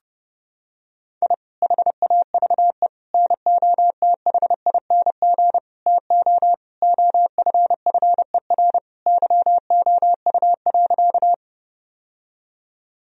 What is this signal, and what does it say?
Morse code, heard as a keyed tone, about 30 wpm, 700 Hz. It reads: I HAVE NOTHING TO OFFER YOU.
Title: Sentences from Top 1000 Words - Review Part 4 - 30wpm